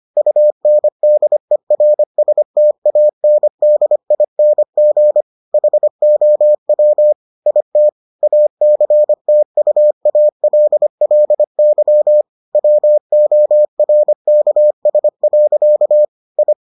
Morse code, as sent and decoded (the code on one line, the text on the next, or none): ..- -. -.. .-. ... - .- -. -.. .. -. --. / .... --- .-- / .. - / .- -.-. - ..- .- .-.. .-.. -.-- / .-- --- .-. -.- ... .-.-.- / ..
UNDERSTANDING HOW IT ACTUALLY WORKS. I